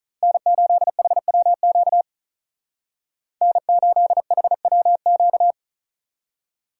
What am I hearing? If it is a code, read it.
Morse code, heard as a keyed tone, about 44 wpm, 700 Hz. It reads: N8HWQ N8HWQ